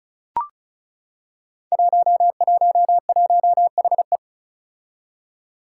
Morse code, heard as a keyed tone, about 36 wpm, 700 Hz. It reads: 111HE
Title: Five Challenging Letter-Number Combinations 35wpm